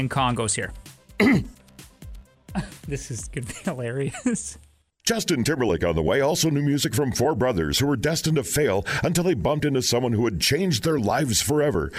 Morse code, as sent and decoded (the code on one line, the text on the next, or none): none